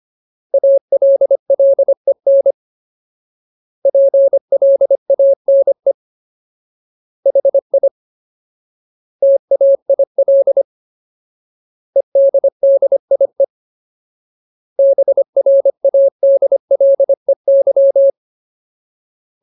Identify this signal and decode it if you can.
Morse code, heard as a keyed tone, about 25 wpm, 550 Hz. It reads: ALLEN PLANE HI TAIL EDDIE BRADLEY
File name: Ultimate Word List / 25WPM Ultimate